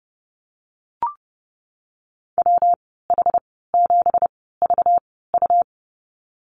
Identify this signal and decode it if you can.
Morse code, heard as a keyed tone, about 30 wpm, spaced as for 17 wpm, 700 Hz. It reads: WH7VU
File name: Lesson 37g - Letter x - Call Signs 30wpm